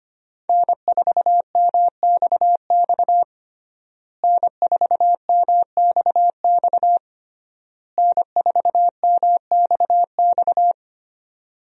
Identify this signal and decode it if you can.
Morse code, heard as a keyed tone, about 25 wpm, 700 Hz. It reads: N4MXX N4MXX N4MXX
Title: USA Call Signs - 5 Characters - Repeated 3x - 25wpm